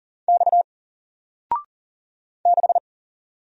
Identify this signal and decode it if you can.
Morse code, heard as a keyed tone, about 40 wpm, 700 Hz. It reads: X 6
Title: Single Letter-Number 40wpm